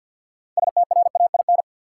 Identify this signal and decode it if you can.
Morse code, heard as a keyed tone, about 52 wpm, 700 Hz. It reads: STRAIN
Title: Six Letter Words - 50wpm